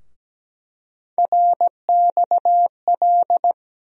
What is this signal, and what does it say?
Morse code, heard as a keyed tone, about 17 wpm, 700 Hz. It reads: RXL
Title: USA Call Sign Postfixes - 3 Letters 17wpm